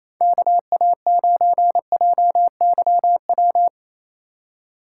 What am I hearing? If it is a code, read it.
Morse code, heard as a keyed tone, about 28 wpm, 700 Hz. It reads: KA9JYW
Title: USA Call Signs - 6 Characters - 4x Character Spacing - 28wpm